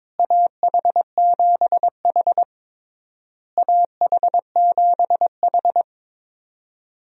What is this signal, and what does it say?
Morse code, heard as a keyed tone, about 22 wpm, 700 Hz. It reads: AH7H AH7H
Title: USA Call Signs - 4 Characters - Repeated 3x - 22wpm